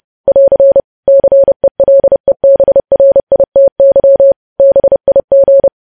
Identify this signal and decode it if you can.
Morse code, heard as a keyed tone, about 30 wpm, 550 Hz. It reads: <AR> CELEBRITY BIG
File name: podcast / News headlines-1x-30x30